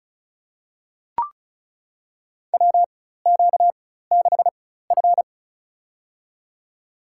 Morse code, heard as a keyed tone, about 36 wpm, spaced as for 16 wpm, 700 Hz. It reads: WQ6F
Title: USA Call Signs - 4 Characters - 4x Character Spacing - 35wpm